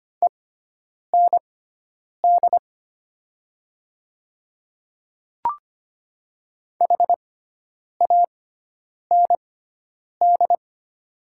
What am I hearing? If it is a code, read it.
Morse code, heard as a keyed tone, about 25 wpm, spaced as for 9 wpm, 700 Hz. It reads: END HAND
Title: 6x Character Spacing - Top 200 Words - 25wpm